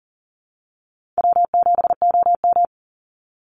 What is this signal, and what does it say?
Morse code, heard as a keyed tone, about 40 wpm, 700 Hz. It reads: W7OM
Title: Easy Call Signs Repeated 3x - 40wpm